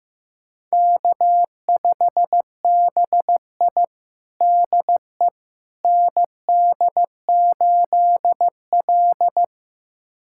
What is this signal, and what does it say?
Morse code, heard as a keyed tone, about 15 wpm, 700 Hz. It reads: K5BI DE ND8L
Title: USA Callsign DE Callsign - 4 Characters - 15wpm